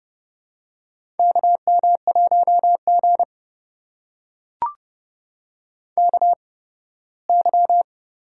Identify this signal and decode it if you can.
Morse code, heard as a keyed tone, about 30 wpm, spaced as for 18 wpm, 700 Hz. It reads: KM1G KY
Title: USA Call Signs - 4 Characters - 8x Character Spacing - 30wpm